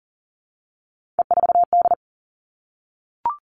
Morse code, heard as a keyed tone, about 39 wpm, 700 Hz. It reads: E4D
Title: Three Challenging Letter-Number Combinations 40wpm